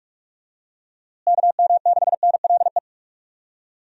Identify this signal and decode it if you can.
Morse code, heard as a keyed tone, about 46 wpm, 700 Hz. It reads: KM6NLE
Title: USA Call Signs - 6 Characters - Repeated 3x - 45wpm